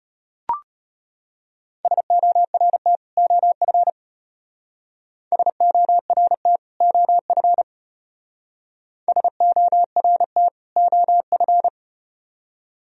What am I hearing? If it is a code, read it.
Morse code, heard as a keyed tone, about 33 wpm, 700 Hz. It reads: SORT OF SORT OF SORT OF